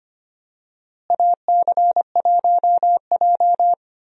Morse code, heard as a keyed tone, about 25 wpm, 700 Hz. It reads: AC1J
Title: USA Call Signs - 4 Characters - Repeated 3x - 25wpm